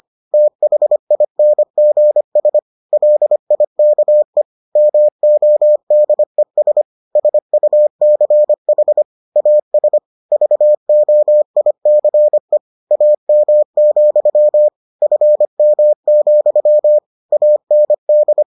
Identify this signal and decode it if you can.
Morse code, heard as a keyed tone, about 25 wpm, 600 Hz. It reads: THINGS LIKE MODES SUCH AS VOICE AM, FM, AND